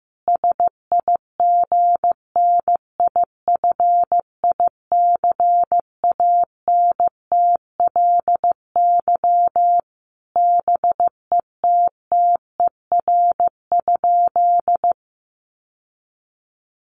Morse code, heard as a keyed tone, about 15 wpm, 700 Hz. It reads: SIGNIFICANTLY BETTER?
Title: Sentences from Top 1500 Words 15wpm